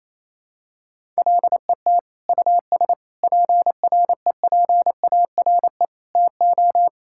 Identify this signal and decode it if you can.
Morse code, heard as a keyed tone, about 28 wpm, 700 Hz. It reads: LET US PREPARE TO
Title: Sentences from Top 1000 Words - Review Part 6 - 28wpm